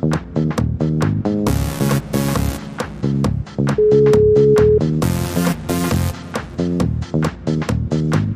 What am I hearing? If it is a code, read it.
Ringback tone (Europe, 425 Hz), with noise.